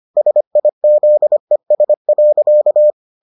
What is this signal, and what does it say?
Morse code, heard as a keyed tone, about 25 wpm, 600 Hz. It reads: SIZES.